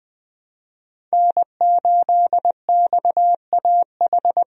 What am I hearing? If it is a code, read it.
Morse code, heard as a keyed tone, about 20 wpm, 700 Hz. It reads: N8XAH
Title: USA Call Signs - 5 Characters - Repeated 3x - 20wpm